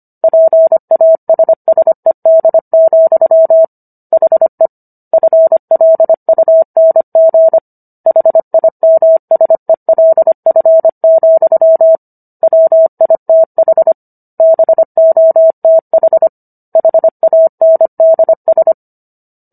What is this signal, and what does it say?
Morse code, heard as a keyed tone, about 25 wpm, 650 Hz. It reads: PASSED, HE FLUNG HIMSELF, WITH BOTH HANDS